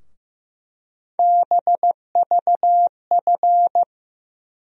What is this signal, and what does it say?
Morse code, heard as a keyed tone, about 15 wpm, 700 Hz. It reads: BVF
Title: USA Call Sign Postfixes - 3 Letters 15wpm